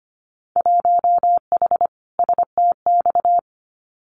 Morse code, heard as a keyed tone, about 25 wpm, 700 Hz. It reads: 1H STX
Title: Winter Field Day Exchanges 25wpm